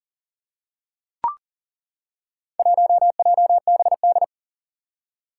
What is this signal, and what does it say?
Morse code, heard as a keyed tone, about 39 wpm, 690 Hz. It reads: 1JBD
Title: Four Challenging Letter-Number Combinations 40wpm